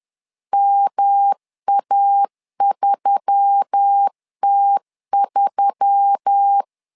Morse code, heard as a keyed tone, about 10 wpm, 800 Hz. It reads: MA3T3